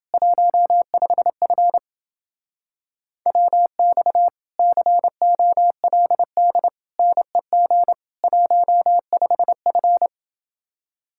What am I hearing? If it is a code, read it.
Morse code, heard as a keyed tone, about 29 wpm, 700 Hz. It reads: 15F WX COLD NEG 15F